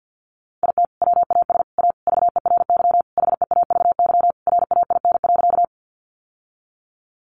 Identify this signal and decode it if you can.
Morse code, heard as a keyed tone, about 50 wpm, 700 Hz. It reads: IT WAS A VERY HEAVY RAIN.
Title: Sentences from Top 1100 Words 50wpm